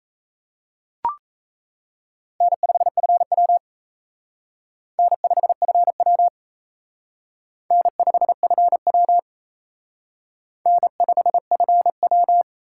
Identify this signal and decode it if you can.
Morse code, heard as a keyed tone, about 35 wpm, 700 Hz. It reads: N5FW N5FW N5FW N5FW